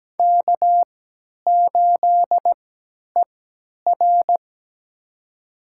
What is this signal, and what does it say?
Morse code, heard as a keyed tone, about 17 wpm, spaced as for 10 wpm, 700 Hz. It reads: K8ER
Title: USA Call Signs - 3x Character Spacing - 17wpm